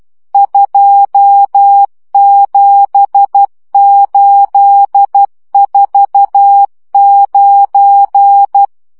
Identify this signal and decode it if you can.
Morse code, heard as a keyed tone, about 12 wpm, 800 Hz. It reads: 27849
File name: morze vizsga 2011-2 10